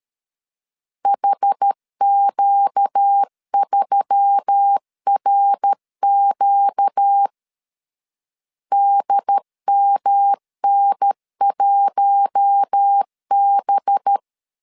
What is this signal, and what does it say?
Morse code, heard as a keyed tone, about 13 wpm, 800 Hz. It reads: HQ3RQ DMN1B
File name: CW / DET / 1 6 1500C S A 12wpm 3440